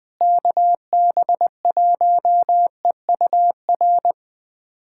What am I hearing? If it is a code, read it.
Morse code, heard as a keyed tone, about 21 wpm, 700 Hz. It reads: KB1EUR
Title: USA Callsign DE Callsign - 6 Characters - 20wpm